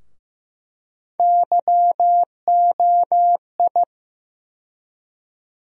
Morse code, heard as a keyed tone, about 16 wpm, 700 Hz. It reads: YOI
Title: USA Call Sign Postfixes - 3 Letters 15wpm